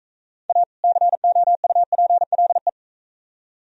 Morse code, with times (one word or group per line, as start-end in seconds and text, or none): 0.5-0.6 A
0.8-2.7 COUPLE